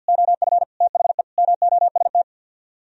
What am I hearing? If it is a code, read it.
Morse code, heard as a keyed tone, about 64 wpm, 700 Hz. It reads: OF THE MOST